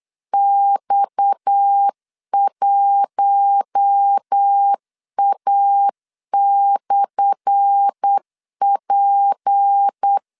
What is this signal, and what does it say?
Morse code, heard as a keyed tone, about 8 wpm, 800 Hz. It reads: X1A/P
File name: LEZIONI / 10 SIGNS / 10B INT 8wpm 3534